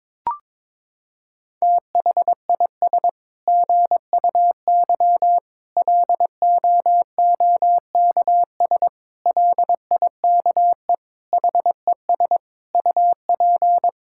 Morse code, heard as a keyed tone, about 21 wpm, 700 Hz. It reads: THIS GUY LOOKS LIKE HES UP
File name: Sets of 9 Words - 22wpm